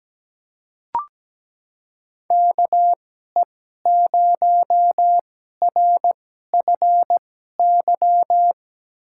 Morse code, heard as a keyed tone, about 17 wpm, spaced as for 12 wpm, 700 Hz. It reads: KE0RFY
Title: USA Call Signs - 6 Characters - 2x Character Spacing - 17wpm